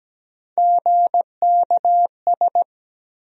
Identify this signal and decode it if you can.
Morse code, heard as a keyed tone, about 17 wpm, 700 Hz. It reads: GKS
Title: USA Call Sign Postfixes - 3 Letters 17wpm